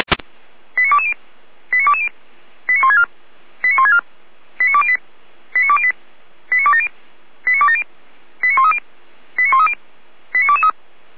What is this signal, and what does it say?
Five-tone selective call (ZVEI), heard as quick runs of five tones, tones of about 80 ms. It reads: 89220 89220 89164 89164 89208 89208 89270 89270 89130 89130 89223